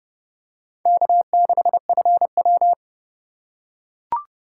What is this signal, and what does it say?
Morse code, heard as a keyed tone, about 29 wpm, 700 Hz. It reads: K6FW